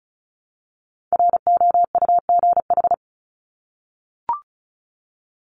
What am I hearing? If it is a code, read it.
Morse code, heard as a keyed tone, about 34 wpm, 700 Hz. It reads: ROUGH
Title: Top 1501-1750 Words 35wpm